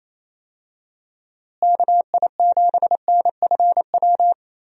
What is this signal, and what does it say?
Morse code, heard as a keyed tone, about 28 wpm, 700 Hz. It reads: KI7NFW
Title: USA Call Signs - 6 Characters - Repeated 2x - 28wpm